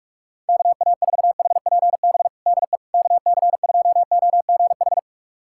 Morse code, heard as a keyed tone, about 44 wpm, 700 Hz. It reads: KA4HPB DE KC2OGH
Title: USA Callsign DE Callsign 45wpm